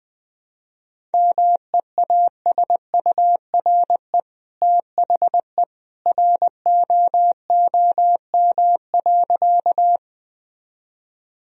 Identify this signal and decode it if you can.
Morse code, heard as a keyed tone, about 20 wpm, 700 Hz. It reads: MEASURE THE ROOM.
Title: Sentences from Top 1000 Words - Review Part 2 - 20wpm